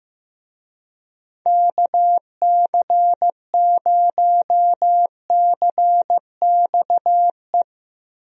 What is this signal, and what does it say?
Morse code, heard as a keyed tone, about 15 wpm, 700 Hz. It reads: KC0CXE
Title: USA Call Signs - 6 Characters - Repeated 2x - 15wpm